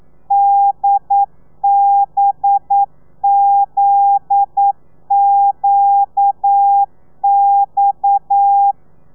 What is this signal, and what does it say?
Morse code, heard as a keyed tone, about 9 wpm, 800 Hz. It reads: DBZQX
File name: telegrafia / i2tfj / Lez9 40